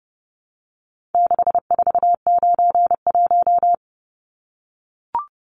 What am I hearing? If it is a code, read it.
Morse code, heard as a keyed tone, about 30 wpm, 700 Hz. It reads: 6491
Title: Numbers - 4-Digit 30wpm